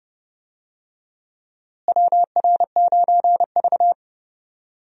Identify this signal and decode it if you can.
Morse code, heard as a keyed tone, about 30 wpm, 700 Hz. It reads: WR9V
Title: USA Call Signs - 4 Characters - Repeated 3x - 30wpm